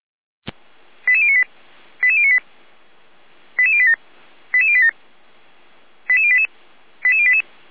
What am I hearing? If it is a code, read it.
Five-tone selective call (ZVEI), heard as quick runs of five tones, tones of about 70 ms. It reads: 88098 88098 88087 88087 88088 88088
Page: SDR/ZVEI